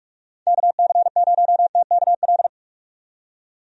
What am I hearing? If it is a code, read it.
Morse code, heard as a keyed tone, about 44 wpm, 700 Hz. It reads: KC0TKL
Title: USA Call Signs - 6 Characters - 45wpm